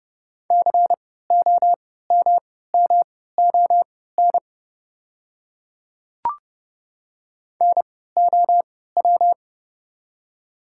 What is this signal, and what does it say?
Morse code, heard as a keyed tone, about 30 wpm, spaced as for 17 wpm, 690 Hz. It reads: COMMON NOW
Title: 3x Character Spacing - Top 500 Words - 30wpm